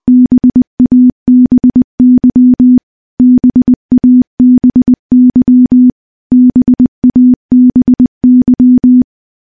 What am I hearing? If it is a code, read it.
Morse code, heard as a keyed tone, about 21 wpm, 250 Hz. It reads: BABY BABY BABY
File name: morse / morse-training-20WPM